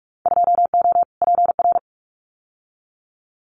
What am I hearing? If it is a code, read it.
Morse code, heard as a keyed tone, about 44 wpm, 700 Hz. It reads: 2O PR
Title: Winter Field Day Exchanges Encore 45wpm